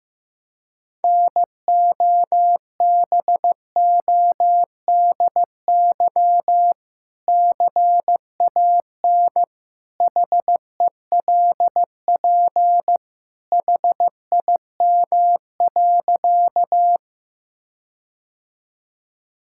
Morse code, heard as a keyed tone, about 15 wpm, 700 Hz. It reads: NOBODY CAN HELP HIM.
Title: Sentences from Top 1600 Words 15wpm